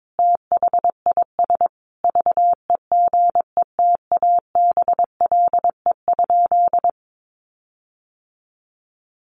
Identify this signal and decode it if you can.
Morse code, heard as a keyed tone, about 22 wpm, 700 Hz. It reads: THIS VEGETABLE?
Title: Sentences from Top 1300 Words 22wpm